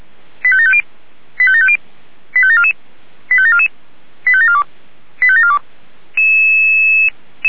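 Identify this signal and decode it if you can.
Five-tone selective call (ZVEI), heard as quick runs of five tones, tones of about 70 ms. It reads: 85750 85750 85744 85744 85742 85742
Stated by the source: ZVEI five-tone selective call received by SDR, from the feuerwehr-orlamuende.org archive.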